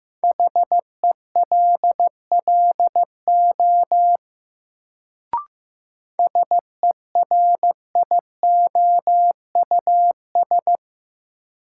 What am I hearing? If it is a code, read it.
Morse code, heard as a keyed tone, about 15 wpm, 700 Hz. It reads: HELLO SERIOUS
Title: Sentences from Top 1100 Words - New Word Drill - 15wpm